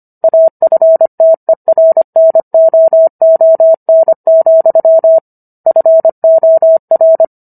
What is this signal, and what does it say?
Morse code, heard as a keyed tone, about 25 wpm, 650 Hz. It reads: AFTERNOON, FOR
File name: Book One / 16-Chapter 16 - 25 WPM